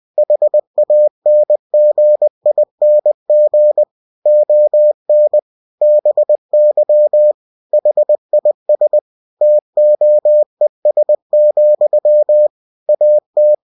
Morse code, heard as a keyed tone, about 20 wpm, 600 Hz. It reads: HANGING ON BY HIS TOES, AT
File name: cw-books / winnie-the-pooh / winnie-20wpm-page-0101